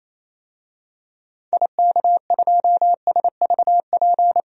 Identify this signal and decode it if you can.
Morse code, heard as a keyed tone, about 28 wpm, 700 Hz. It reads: IK2SVP
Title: DXCC-100 Call Signs - 6 Characters - 28wpm